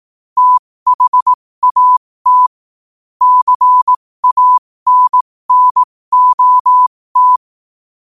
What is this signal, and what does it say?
Morse code, heard as a keyed tone, about 18 wpm, spaced as for 15 wpm, 1000 Hz. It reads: THAT CANNOT
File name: qotdfiles / 15wpm